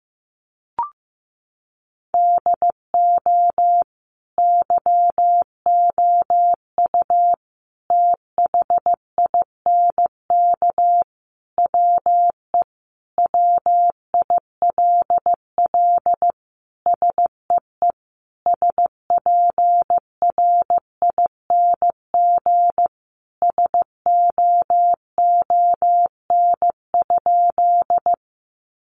Morse code, heard as a keyed tone, about 15 wpm, 700 Hz. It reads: DO YOU THINK WE WILL SEE SPRING SOON?
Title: Sentences from Top 1200 Words 15wpm